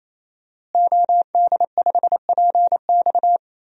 Morse code, heard as a keyed tone, about 28 wpm, 700 Hz. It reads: OD5PX
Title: DXCC-100 Call Signs - 5 Characters - Repeated 3x - 28wpm